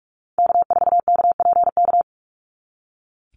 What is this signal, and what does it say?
Morse code, heard as a keyed tone, about 46 wpm, 700 Hz. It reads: K4KPK
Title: NA SOTA Activator Callsigns 45wpm